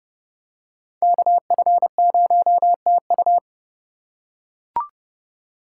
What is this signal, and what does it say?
Morse code, heard as a keyed tone, about 30 wpm, 700 Hz. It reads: KF0TU